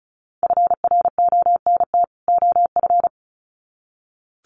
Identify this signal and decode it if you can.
Morse code, heard as a keyed tone, about 34 wpm, 700 Hz. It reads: FRONT OF
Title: Sets of 2 Words 35wpm